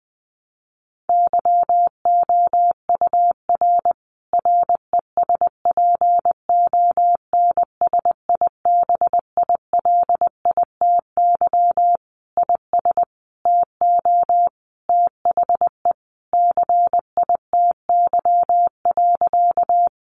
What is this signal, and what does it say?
Morse code, heard as a keyed tone, about 20 wpm, 700 Hz. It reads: YOUR RESPONSIBILITY IS TO THE CITY.
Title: Sentences from Top 1200 Words 20wpm